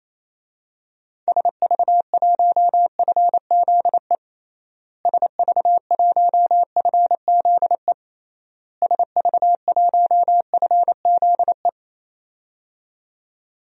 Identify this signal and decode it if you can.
Morse code, heard as a keyed tone, about 28 wpm, 700 Hz. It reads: SV1FZE SV1FZE SV1FZE
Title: DXCC-100 Call Signs - 6 Characters - Repeated 3x - 28wpm